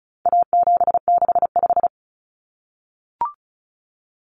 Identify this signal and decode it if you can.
Morse code, heard as a keyed tone, about 34 wpm, 700 Hz. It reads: A765
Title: Common Cut Numbers - 4-Digit 35wpm